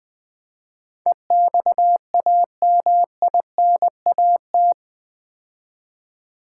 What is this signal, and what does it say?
Morse code, heard as a keyed tone, about 20 wpm, 700 Hz. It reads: EXAMINAT
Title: Build A Word - 4 Letters At A Time - 11-Letter Words - 20wpm